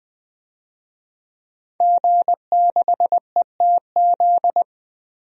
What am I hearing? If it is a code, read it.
Morse code, heard as a keyed tone, about 20 wpm, 700 Hz. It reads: G6ETZ
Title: DXCC-100 Call Signs - 5 Characters - 20wpm